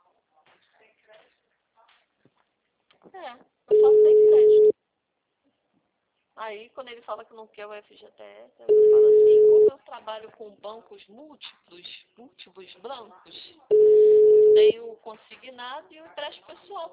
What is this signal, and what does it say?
Ringback tone (Europe, 425 Hz), with speech.